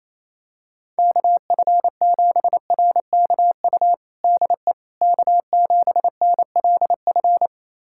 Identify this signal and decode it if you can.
Morse code, heard as a keyed tone, about 28 wpm, 700 Hz. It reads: KF7RKU DE K7NLF
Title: USA Callsign DE Callsign 28wpm